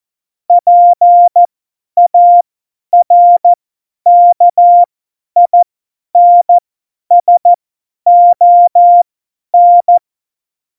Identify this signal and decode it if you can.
Morse code, heard as a keyed tone, about 14 wpm, spaced as for 10 wpm, 700 Hz. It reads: PARKINSON
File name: Morsekurs / Witze / Witz 01. Oktober 2019, 50 F70